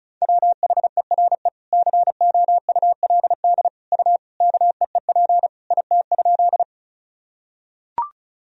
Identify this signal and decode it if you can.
Morse code, heard as a keyed tone, about 36 wpm, 700 Hz. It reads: WHERE COULD U KEEP IT?
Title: Sentences from Top 300 Words and Abbreviations 35wpm